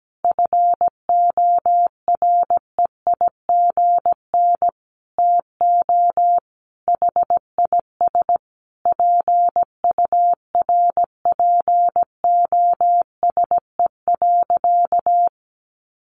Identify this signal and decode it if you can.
Morse code, heard as a keyed tone, about 17 wpm, 700 Hz. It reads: FOREIGN TO HIS PURPOSE.
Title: Sentences from Top 1400 Words 17wpm